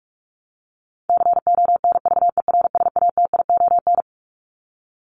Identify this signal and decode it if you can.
Morse code, heard as a keyed tone, about 44 wpm, 700 Hz. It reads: CONVERSATION
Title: Top 1751-2000 Words 45wpm